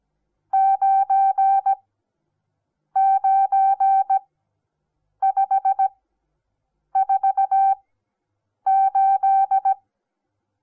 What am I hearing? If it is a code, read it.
Morse code, heard as a keyed tone, about 17 wpm, spaced as for 7 wpm, 780 Hz. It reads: 99548
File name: Onlinelehrgang / morsen / DJ4UF Morsen08